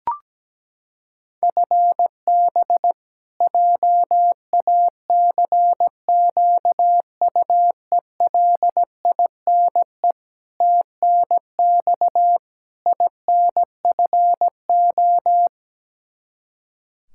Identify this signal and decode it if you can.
Morse code, heard as a keyed tone, about 17 wpm, 700 Hz. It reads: FB JACQUELINE TNX INFO
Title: Practice QSO - Exchange 3 - More Info - 17wpm